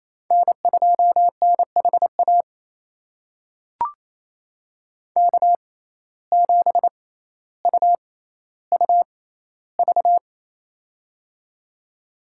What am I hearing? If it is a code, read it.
Morse code, heard as a keyed tone, about 28 wpm, spaced as for 14 wpm, 700 Hz. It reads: N2NHA K7UUV
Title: USA Call Signs - 5 Characters - 6x Character Spacing - 28wpm